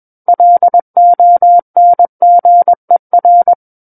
Morse code, heard as a keyed tone, about 21 wpm, 700 Hz. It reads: LONGER
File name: Book One / 12-Chapter 12 - 21 WPM